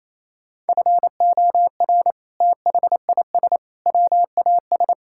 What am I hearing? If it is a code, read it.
Morse code, heard as a keyed tone, about 28 wpm, 700 Hz. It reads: FOR THIS WAS
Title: Sentences from Top 1000 Words - Review Part 5 - 28wpm